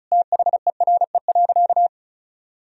Morse code, heard as a keyed tone, about 36 wpm, 700 Hz. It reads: THERE.
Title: Sentences from Top 1000 Words - Review Part 4 - 35wpm